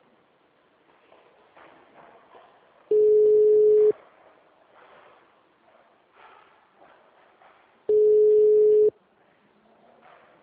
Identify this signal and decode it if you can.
Ringback tone (Europe, 425 Hz).
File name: out- -2015- - - .41238